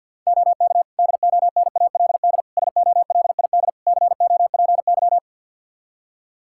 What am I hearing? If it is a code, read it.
Morse code, heard as a keyed tone, about 52 wpm, 700 Hz. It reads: OK DONALD SOLID COPY
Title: Practice QSO - Exchange 3 - More Info - 50wpm